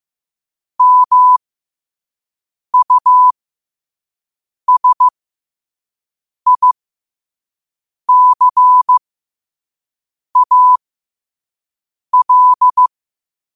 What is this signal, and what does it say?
Morse code, heard as a keyed tone, about 15 wpm, spaced as for 5 wpm, 1000 Hz. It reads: MUSICAL
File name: 5wpm